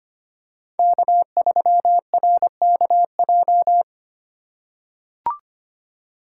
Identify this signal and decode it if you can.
Morse code, heard as a keyed tone, about 25 wpm, 700 Hz. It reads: K3RKJ
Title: USA Call Signs - 5 Characters - 4x Character Spacing - 25wpm